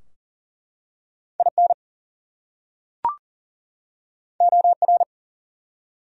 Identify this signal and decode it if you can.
Morse code, heard as a keyed tone, about 39 wpm, 700 Hz. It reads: IN OR
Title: Top 100 Words 40wpm